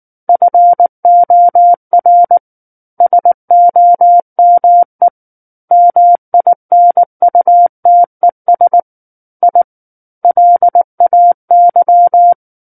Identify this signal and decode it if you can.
Morse code, heard as a keyed tone, about 19 wpm, 700 Hz. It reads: FOR SOME MINUTES I LAY
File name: Book One / 10-Chapter 10 - 19 WPM